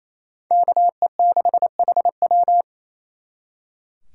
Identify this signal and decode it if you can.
Morse code, heard as a keyed tone, about 36 wpm, 700 Hz. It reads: KE6HW